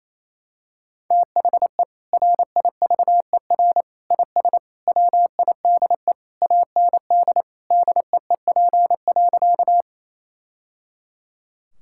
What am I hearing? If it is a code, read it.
Morse code, heard as a keyed tone, about 28 wpm, 700 Hz. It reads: THE RIVER IS WIDE AND DEEP.
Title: Sentences from Top 1000 Words - Review Part 6 - 28wpm